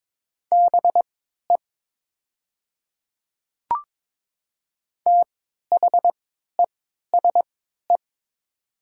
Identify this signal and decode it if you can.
Morse code, heard as a keyed tone, about 22 wpm, spaced as for 12 wpm, 700 Hz. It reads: BE THESE
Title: 3x Character Spacing - Top 300 Words - 22wpm